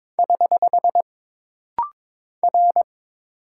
Morse code, heard as a keyed tone, about 22 wpm, 700 Hz. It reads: <HH> R
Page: This rapid-fire timing is geared to push your ICR (Instant Character Recognition) skills.